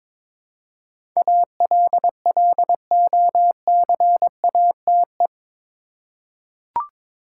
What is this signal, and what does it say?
Morse code, heard as a keyed tone, about 22 wpm, 700 Hz. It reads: ALLOCATE
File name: practice / 22wpm / Eight Letter Words - 22wpm